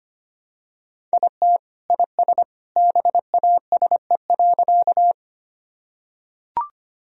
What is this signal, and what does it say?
Morse code, heard as a keyed tone, about 25 wpm, 700 Hz. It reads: IT IS BASE.
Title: Sentences from Top 1000 Words - Review Part 1 - 25wpm